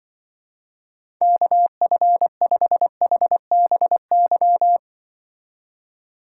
Morse code, heard as a keyed tone, about 24 wpm, 700 Hz. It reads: KF5HBY
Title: Speed Racing - USA Call Signs 20wpm